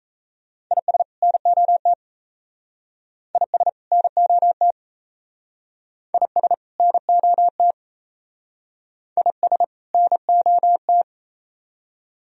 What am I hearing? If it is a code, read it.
Morse code, heard as a keyed tone, about 34 wpm, 700 Hz. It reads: IS NOT IS NOT IS NOT IS NOT
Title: Speed-Racing - Sets of 2 Words 28wpm